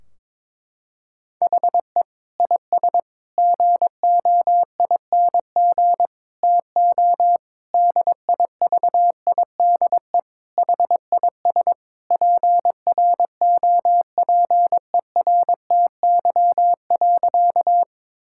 Morse code, heard as a keyed tone, about 22 wpm, 700 Hz. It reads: HE IS GOING TO DIVIDE HIS PROPERTY.
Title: Sentences from Top 1600 Words 22wpm